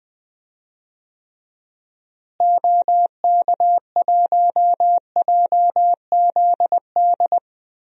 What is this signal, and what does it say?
Morse code, heard as a keyed tone, about 20 wpm, 700 Hz. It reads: OK1JZD
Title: DXCC-100 Call Signs - 6 Characters - Repeated 2x - 20wpm